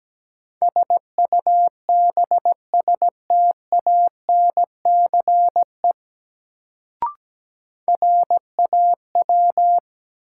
Morse code, heard as a keyed tone, about 17 wpm, 700 Hz. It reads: SUBSTANCE RAW
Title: Sentences from Top 1500 Words - New Word Drill - 17wpm